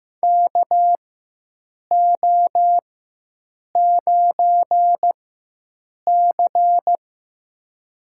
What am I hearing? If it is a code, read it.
Morse code, heard as a keyed tone, about 15 wpm, spaced as for 7 wpm, 700 Hz. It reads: KO9C
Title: USA Call Signs - 4x Character Spacing - 15wpm